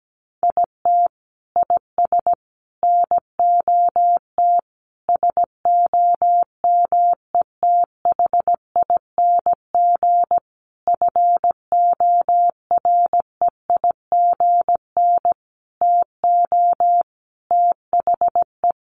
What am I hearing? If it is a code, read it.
Morse code, heard as a keyed tone, about 17 wpm, 700 Hz. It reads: IT IS NOT SOMETHING FOREIGN TO THE
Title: Sentences from Top 1400 Words 17wpm